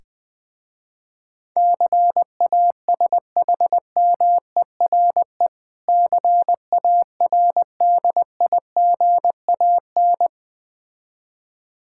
Morse code, heard as a keyed tone, about 20 wpm, 700 Hz. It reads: CASHMERE CARDIGAN
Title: Difficult Words with Hints - 20wpm